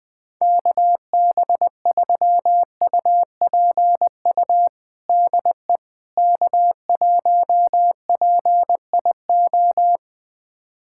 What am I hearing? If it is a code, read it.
Morse code, heard as a keyed tone, about 20 wpm, 700 Hz. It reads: KB3UPU DE K1PIO
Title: USA Callsign DE Callsign 20wpm